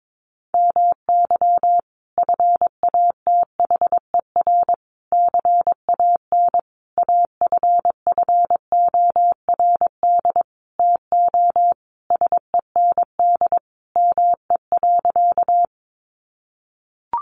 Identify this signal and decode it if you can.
Morse code, heard as a keyed tone, about 22 wpm, 700 Hz. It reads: MY FATHER CAN AFFORD TO SEND ME.
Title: Sentences from Top 1500 Words 22wpm